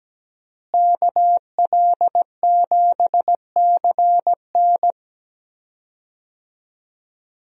Morse code, heard as a keyed tone, about 17 wpm, 700 Hz. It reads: KL7CN
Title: USA Call Signs - 5 Characters - Repeated 2x - 17wpm